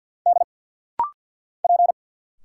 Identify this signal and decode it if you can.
Morse code, heard as a keyed tone, about 50 wpm, 690 Hz. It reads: D P
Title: Single Letters - Rapid-Fire 50wpm